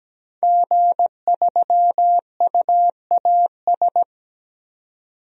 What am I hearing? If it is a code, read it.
Morse code, heard as a keyed tone, about 17 wpm, 700 Hz. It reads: G3UAS
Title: DXCC-100 Call Signs - 5 Characters - 17wpm